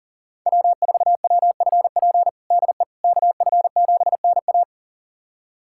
Morse code, heard as a keyed tone, about 40 wpm, 700 Hz. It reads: W4WFP DE KF7NA
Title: USA Callsign DE Callsign - 5 Characters - 40wpm